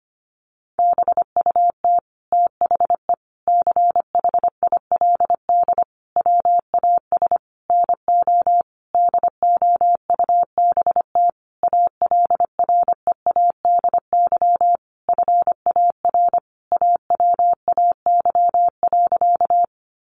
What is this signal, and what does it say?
Morse code, heard as a keyed tone, about 25 wpm, 700 Hz. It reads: BUT THE CHILD WAS NO DOUBT ALREADY FAR AWAY.
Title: Sentences from Top 1500 Words 25wpm